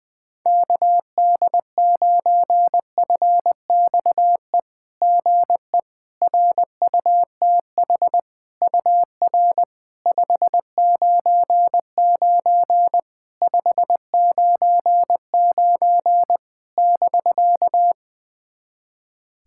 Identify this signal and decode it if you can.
Morse code, heard as a keyed tone, about 20 wpm, 660 Hz. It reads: KD9FXE GE RUTH UR 599 599 <BK>
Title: SOTA QSO Practice 20wpm